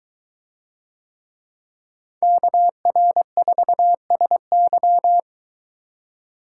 Morse code, heard as a keyed tone, about 23 wpm, 700 Hz. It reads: KR4SY